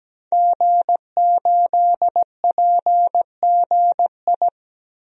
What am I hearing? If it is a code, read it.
Morse code, heard as a keyed tone, about 17 wpm, 700 Hz. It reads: G8PGI
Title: DXCC-100 Call Signs - 5 Characters - 17wpm